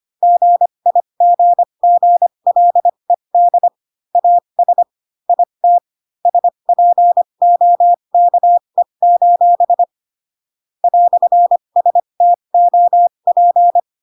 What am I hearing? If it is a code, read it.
Morse code, heard as a keyed tone, about 25 wpm, spaced as for 22 wpm, 700 Hz. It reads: GIGGLED AS IT SPOKE: "STOP